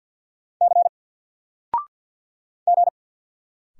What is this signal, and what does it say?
Morse code, heard as a keyed tone, about 50 wpm, 700 Hz. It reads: C G
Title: Single Letter-Number 50wpm